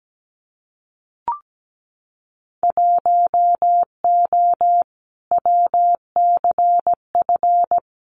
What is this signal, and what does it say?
Morse code, heard as a keyed tone, about 17 wpm, 690 Hz. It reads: 1O WCF
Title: Winter Field Day Exchanges 17wpm